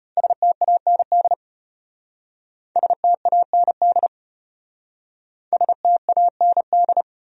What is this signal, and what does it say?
Morse code, heard as a keyed tone, about 34 wpm, 700 Hz. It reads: STAND STAND STAND